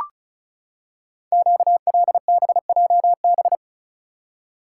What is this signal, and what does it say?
Morse code, heard as a keyed tone, about 36 wpm, 700 Hz. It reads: QLBJB
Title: Code Groups - 5 Letters - 35wpm